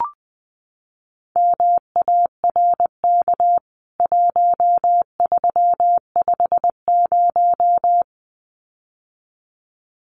Morse code, heard as a keyed tone, about 20 wpm, 700 Hz. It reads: MARK 1350
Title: CWT Exchange 20wpm